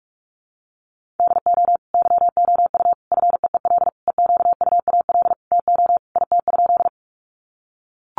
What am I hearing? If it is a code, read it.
Morse code, heard as a keyed tone, about 44 wpm, 700 Hz. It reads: DO YOU FEEL EQUAL TO IT?
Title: Sentences from Top 1400 Words 45wpm